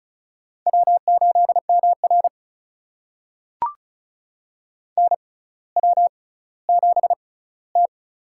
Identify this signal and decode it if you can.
Morse code, heard as a keyed tone, about 36 wpm, spaced as for 18 wpm, 700 Hz. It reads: W8MR NW7T
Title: USA Call Signs - 4 Characters - 6x Character Spacing - 35wpm